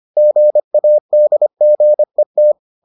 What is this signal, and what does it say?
Morse code, heard as a keyed tone, about 25 wpm, 600 Hz. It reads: GADGET